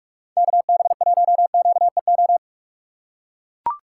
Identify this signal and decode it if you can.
Morse code, heard as a keyed tone, about 43 wpm, 700 Hz. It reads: KB1QEO
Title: USA Call Signs - 4x Character Spacing - 45wpm